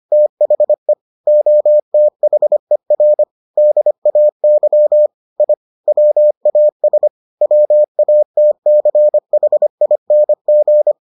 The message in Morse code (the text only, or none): THE OTHER DAY I WAS WATCHING